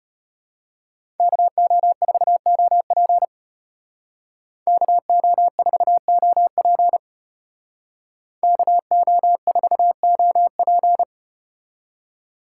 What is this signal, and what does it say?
Morse code, heard as a keyed tone, about 33 wpm, 700 Hz. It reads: KO4OP KO4OP KO4OP